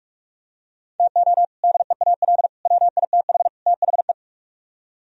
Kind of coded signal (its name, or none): Morse code